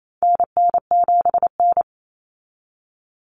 Morse code, heard as a keyed tone, about 28 wpm, 700 Hz. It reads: NN7N
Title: Common Cut Numbers - 4-Digit 28wpm